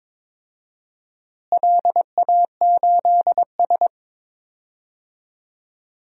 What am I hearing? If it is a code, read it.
Morse code, heard as a keyed tone, about 22 wpm, 700 Hz. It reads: LA8S
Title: DXCC-100 Call Signs - 4 Characters - Repeated 2x - 22wpm